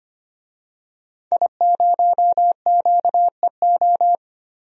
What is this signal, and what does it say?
Morse code, heard as a keyed tone, about 25 wpm, 700 Hz. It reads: I0QEO
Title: DXCC-100 Call Signs - 5 Characters - Repeated 3x - 25wpm